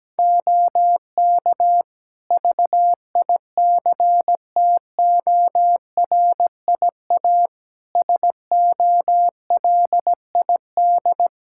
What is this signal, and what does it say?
Morse code, heard as a keyed tone, about 17 wpm, 700 Hz. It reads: OK VICTORIA SOLID
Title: Practice QSO - Exchange 3 - More Info - 17wpm